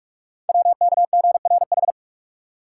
Morse code, heard as a keyed tone, about 44 wpm, 690 Hz. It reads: WKGRH
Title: Code Groups - 5 Letters - 45wpm